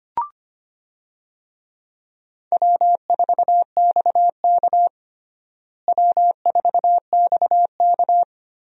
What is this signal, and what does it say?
Morse code, heard as a keyed tone, about 25 wpm, 700 Hz. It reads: W4XK W4XK
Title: USA Call Signs - 4 Characters - Repeated 3x - 25wpm